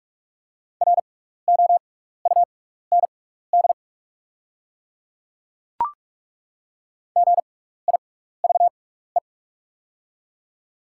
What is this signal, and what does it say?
Morse code, heard as a keyed tone, about 48 wpm, spaced as for 16 wpm, 690 Hz. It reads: ROUND GIVE
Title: 6x Character Spacing - Top 200 Words - 45wpm